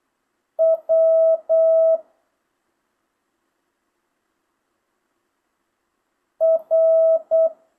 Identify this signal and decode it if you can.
Morse code, heard as a keyed tone, about 8 wpm, 650 Hz. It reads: W R
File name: jh8cbh / 2025cw